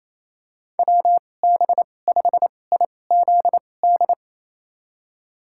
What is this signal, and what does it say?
Morse code, heard as a keyed tone, about 28 wpm, spaced as for 20 wpm, 700 Hz. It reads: WB5IZD
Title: USA Call Signs - 2x Character Spacing - 28wpm